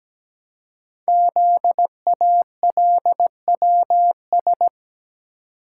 Morse code, heard as a keyed tone, about 17 wpm, 700 Hz. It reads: ZALWS
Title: Code Groups - 5 Letters - 17wpm